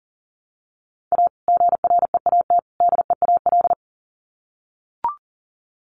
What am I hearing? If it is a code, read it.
Morse code, heard as a keyed tone, about 40 wpm, 700 Hz. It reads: A GREAT DEAL
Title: Sets of 3 Words 40wpm